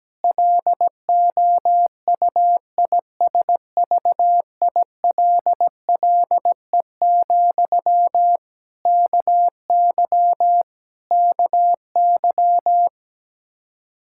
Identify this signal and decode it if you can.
Morse code, heard as a keyed tone, about 17 wpm, 700 Hz. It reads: LOUISVILLE, KY KY
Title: K1USN SST QSO Practice 17wpm